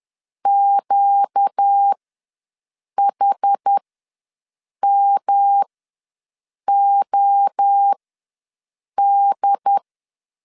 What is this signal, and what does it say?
Morse code, heard as a keyed tone, about 10 wpm, spaced as for 6 wpm, 800 Hz. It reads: QHMOD
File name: LEZIONI / 5 LBVC / 5 L ws 10wpm 3636